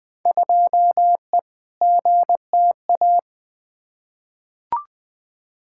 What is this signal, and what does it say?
Morse code, heard as a keyed tone, about 20 wpm, 700 Hz. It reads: 2E GTA
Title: Field Day Exchanges Encore - 20wpm